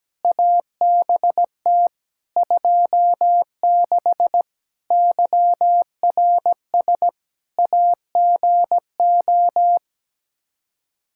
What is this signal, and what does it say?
Morse code, heard as a keyed tone, about 17 wpm, 700 Hz. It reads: ABT 26 YRS AGO
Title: Practice QSO - Exchange 3 - More Info - 17wpm